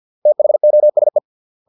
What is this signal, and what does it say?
Morse code, heard as a keyed tone, about 50 wpm, 600 Hz. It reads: THOSE